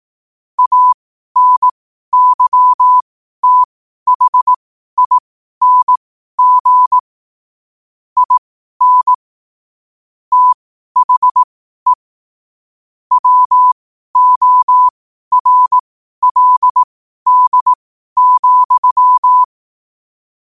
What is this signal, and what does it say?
Morse code, heard as a keyed tone, about 18 wpm, spaced as for 12 wpm, 1000 Hz. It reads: ANYTHING IN THE WORLD,
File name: qotdfiles / 12wpm